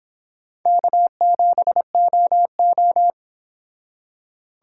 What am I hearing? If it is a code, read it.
Morse code, heard as a keyed tone, about 31 wpm, 700 Hz. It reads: K7OO